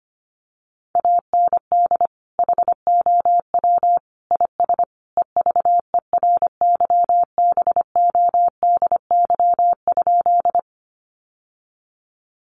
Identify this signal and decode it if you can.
Morse code, heard as a keyed tone, about 25 wpm, 700 Hz. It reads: AND HOW IS EVERYBODY?
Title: Sentences from Top 1400 Words 25wpm